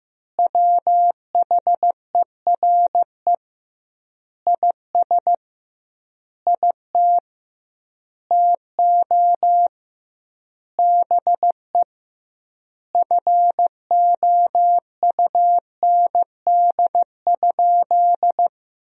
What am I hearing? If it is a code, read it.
Morse code, heard as a keyed tone, about 15 wpm, 700 Hz. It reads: WHERE IS IT TO BE FOUND?
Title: 2x Word-Spacing - Sentences from Top 300 Words - 15wpm